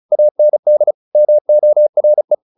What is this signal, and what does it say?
Morse code, heard as a keyed tone, about 34 wpm, 600 Hz. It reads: AND MORE
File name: the old man and the sea / the old man and the sea 1 35wpm